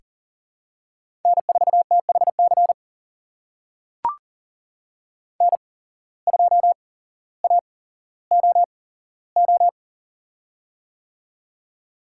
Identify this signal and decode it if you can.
Morse code, heard as a keyed tone, about 38 wpm, spaced as for 17 wpm, 700 Hz. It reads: N4THC N2AOO
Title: USA Call Signs - 5 Characters - 8x Character Spacing - 40wpm